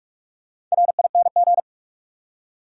Morse code, heard as a keyed tone, about 43 wpm, 700 Hz. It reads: RING
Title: Four Letter Words - 45wpm